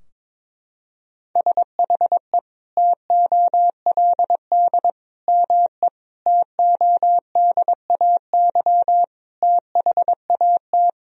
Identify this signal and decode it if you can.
Morse code, heard as a keyed tone, about 22 wpm, 700 Hz. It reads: SHE TOLD ME TODAY THAT
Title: Sentences from Top 1000 Words - Review Part 4 - 22wpm